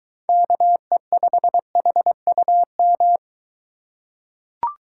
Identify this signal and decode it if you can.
Morse code, heard as a keyed tone, about 23 wpm, 700 Hz. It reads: KE5HUM